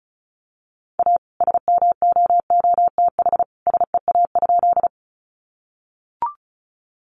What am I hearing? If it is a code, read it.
Morse code, heard as a keyed tone, about 34 wpm, 700 Hz. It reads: A SMOOTH SEA?
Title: Sentences from Top 1400 Words 35wpm